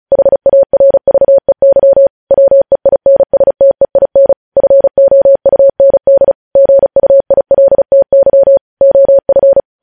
Morse code, heard as a keyed tone, about 36 wpm, 550 Hz. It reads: HARVEY WEINSTEIN FOUND GUILTY OF
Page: news headlines in morse code at 35 wpm